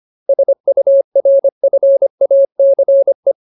Morse code, heard as a keyed tone, about 25 wpm, 550 Hz. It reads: SURFACE
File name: Ultimate Word List / 25WPM Ultimate